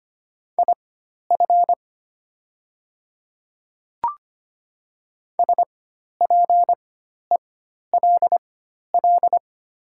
Morse code, heard as a keyed tone, about 25 wpm, spaced as for 12 wpm, 700 Hz. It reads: IF SPELL
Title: 4x Character Spacing - Top 200 Words - 25wpm